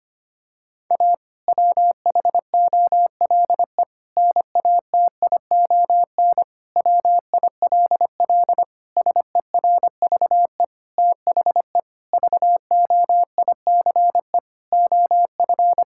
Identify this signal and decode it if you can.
Morse code, heard as a keyed tone, about 25 wpm, 700 Hz. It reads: A WHOLE NATION WILL SERVE THE VOICE OF
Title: Sentences from Top 1000 Words - Review Part 6 - 25wpm